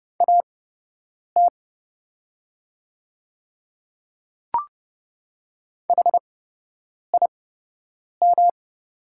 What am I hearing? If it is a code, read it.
Morse code, heard as a keyed tone, about 30 wpm, spaced as for 8 wpm, 700 Hz. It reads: AT HIM
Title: 8x Character Spacing - Top 100 Words - 30wpm